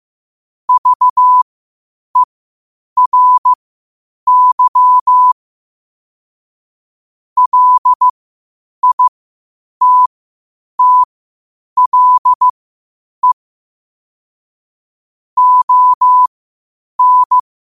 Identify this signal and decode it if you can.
Morse code, heard as a keyed tone, about 15 wpm, spaced as for 8 wpm, 1000 Hz. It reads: VERY LITTLE ON
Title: 8 WPM morse code quotes for Fri, 01 Mar 2024